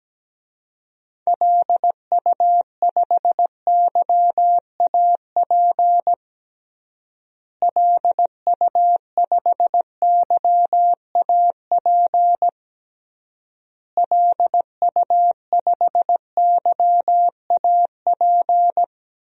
Morse code, heard as a keyed tone, about 17 wpm, 700 Hz. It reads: LU5YAP LU5YAP LU5YAP
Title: DXCC-100 Call Signs - 6 Characters - Repeated 3x - 17wpm